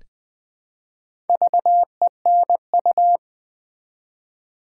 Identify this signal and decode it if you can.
Morse code, heard as a keyed tone, about 20 wpm, 690 Hz. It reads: VENU
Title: Build A Word - 1 Letter At A Time - 5-Letter Words - 20wpm